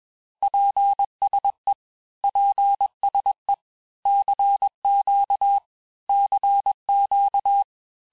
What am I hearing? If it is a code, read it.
Morse code, heard as a keyed tone, about 21 wpm, 800 Hz. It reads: PSE PSE CQ CQ
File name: cours-du-lundi files / Cours-1586-texte-abreviations-vitesse-15mpm-puis-20mpm